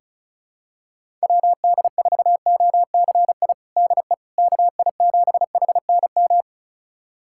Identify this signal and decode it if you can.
Morse code, heard as a keyed tone, about 36 wpm, 700 Hz. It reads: WD4OCI DE KI7HNM
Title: USA Callsign DE Callsign - 6 Characters - 35wpm